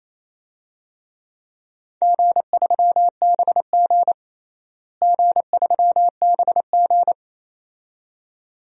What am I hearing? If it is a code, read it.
Morse code, heard as a keyed tone, about 28 wpm, 700 Hz. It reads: G3BG G3BG
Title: DXCC-100 Call Signs - 4 Characters - Repeated 2x - 28wpm